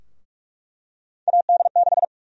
Morse code, heard as a keyed tone, about 43 wpm, 700 Hz. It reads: AD6
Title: USA Call Sign Prefixes - 45wpm